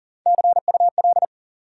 Morse code, heard as a keyed tone, about 40 wpm, 700 Hz. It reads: CUL
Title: Top 90 QSO Elements 40wpm